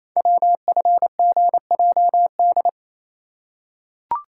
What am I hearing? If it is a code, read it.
Morse code, heard as a keyed tone, about 27 wpm, 690 Hz. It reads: WFGJD
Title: Code Groups - 5 Letter:Numbers - 28wpm